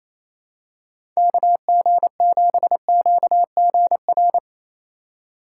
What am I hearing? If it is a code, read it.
Morse code, heard as a keyed tone, about 27 wpm, 700 Hz. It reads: KG7QGR
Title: USA Call Signs - 6 Characters - 4x Character Spacing - 28wpm